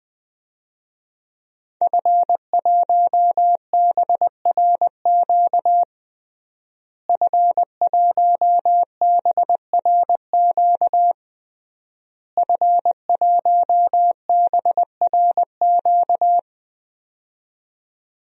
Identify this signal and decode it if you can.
Morse code, heard as a keyed tone, about 20 wpm, 700 Hz. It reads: F1BRQ F1BRQ F1BRQ
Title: DXCC-100 Call Signs - 5 Characters - Repeated 3x - 20wpm